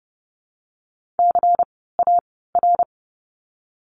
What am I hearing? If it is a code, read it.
Morse code, heard as a keyed tone, about 30 wpm, spaced as for 17 wpm, 700 Hz. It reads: CAR
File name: Lesson 16f - Letter c - Words 30wpm